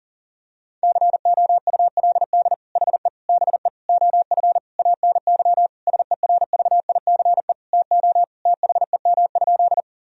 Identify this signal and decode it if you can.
Morse code, heard as a keyed tone, about 40 wpm, 700 Hz. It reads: COULD HE BE OF ANY SERVICE TO THEM?
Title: Sentences from Top 1000 Words - Review Part 4 - 40wpm